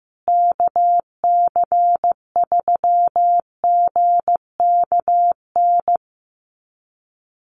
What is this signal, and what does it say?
Morse code, heard as a keyed tone, about 15 wpm, 700 Hz. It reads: KC3GKN
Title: USA Call Signs Repeated 2x 15wpm